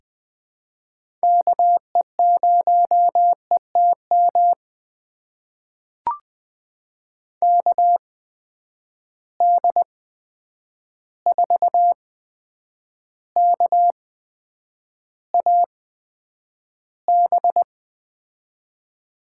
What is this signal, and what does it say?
Morse code, heard as a keyed tone, about 21 wpm, spaced as for 9 wpm, 700 Hz. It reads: KE0ETM KD4KAB
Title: USA Call Signs - 6 Characters - 8x Character Spacing - 20wpm